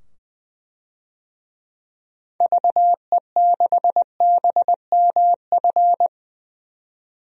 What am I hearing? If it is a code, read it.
Morse code, heard as a keyed tone, about 21 wpm, 700 Hz. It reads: VE6BMF
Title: DXCC-100 Call Signs - 6 Characters - 20wpm